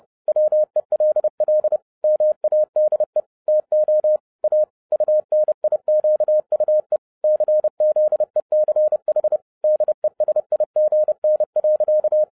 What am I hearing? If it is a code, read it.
Morse code, heard as a keyed tone, about 30 wpm, 610 Hz. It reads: WELL MADE TO A UNIQUE CZECH DESIGN.
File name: 2011-10-mp3 / n10-Czech Army Morse Keys 30wpm